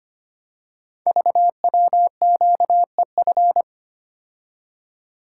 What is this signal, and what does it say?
Morse code, heard as a keyed tone, about 25 wpm, 700 Hz. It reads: VWQEF
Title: Code Groups - 5 Letters - 25wpm